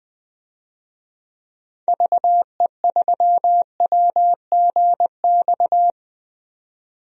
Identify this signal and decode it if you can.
Morse code, heard as a keyed tone, about 20 wpm, 700 Hz. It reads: VE3WGX
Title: DXCC-100 Call Signs - 6 Characters - 20wpm